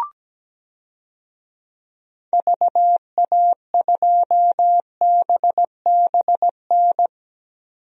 Morse code, heard as a keyed tone, about 17 wpm, 700 Hz. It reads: VA2BBN
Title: DXCC-100 Call Signs - 6 Characters - 17wpm